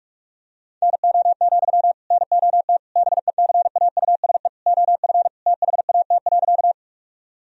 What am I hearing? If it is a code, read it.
Morse code, heard as a keyed tone, about 46 wpm, 700 Hz. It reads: NO, NOT BECAUSE OF THAT.
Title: Sentences from Top 1000 Words - Review Part 3 - 45wpm